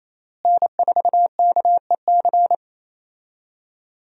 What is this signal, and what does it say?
Morse code, heard as a keyed tone, about 28 wpm, 700 Hz. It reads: N4KEC
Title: USA Call Signs - 5 Characters - Repeated 3x - 28wpm